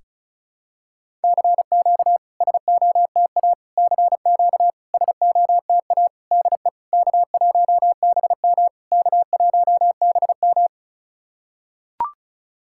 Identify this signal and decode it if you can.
Morse code, heard as a keyed tone, about 34 wpm, 690 Hz. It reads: CQ SOTA CQ SOTA DE K1BM K1BM
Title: SOTA QSO Practice 35wpm